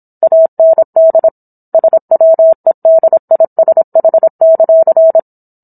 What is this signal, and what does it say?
Morse code, heard as a keyed tone, about 27 wpm, 650 Hz. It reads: AND SWEDISH;